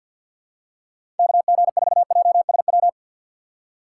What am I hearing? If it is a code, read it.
Morse code, heard as a keyed tone, about 53 wpm, 700 Hz. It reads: KG4JSW
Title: USA Call Signs - 6 Characters - 4x Character Spacing - 50wpm